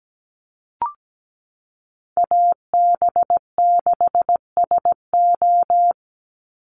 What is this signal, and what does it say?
Morse code, heard as a keyed tone, about 17 wpm, 700 Hz. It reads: AB6SO
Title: NA SOTA Chaser Callsigns 17wpm